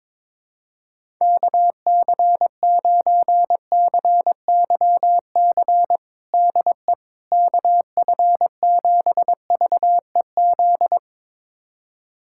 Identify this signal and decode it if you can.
Morse code, heard as a keyed tone, about 22 wpm, 700 Hz. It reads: KC9CYC DE KF7VEZ
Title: USA Callsign DE Callsign 22wpm